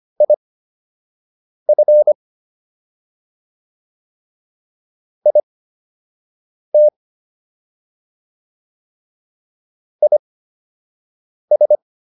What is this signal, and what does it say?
Morse code, heard as a keyed tone, about 25 wpm, spaced as for 6 wpm, 600 Hz. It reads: IF IT IS